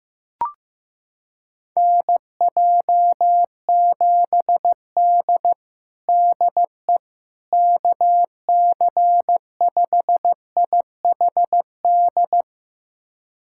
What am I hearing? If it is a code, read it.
Morse code, heard as a keyed tone, about 15 wpm, 700 Hz. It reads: NJ7D DE KC5IHD
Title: USA Callsign DE Callsign 15wpm